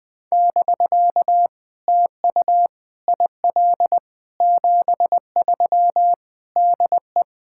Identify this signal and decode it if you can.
Morse code, heard as a keyed tone, about 20 wpm, 700 Hz. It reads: <BK> TU IL 73 DE
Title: POTA QSO Practice - 20wpm